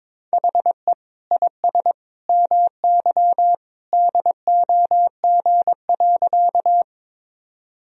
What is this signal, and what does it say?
Morse code, heard as a keyed tone, about 22 wpm, 700 Hz. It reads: HE IS MY DOG.
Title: Sentences from Top 1000 Words - Review Part 2 - 22wpm